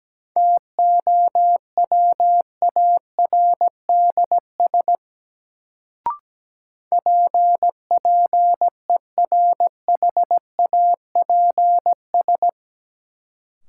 Morse code, heard as a keyed tone, about 17 wpm, 700 Hz. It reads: TOWARDS PPERHAPS
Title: Sentences from Top 1100 Words - New Word Drill Alt - 17wpm